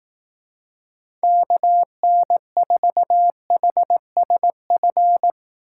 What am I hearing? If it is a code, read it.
Morse code, heard as a keyed tone, about 18 wpm, 700 Hz. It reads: KN4HSF
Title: Speed Racing - USA 2x3 Call Signs 15wpm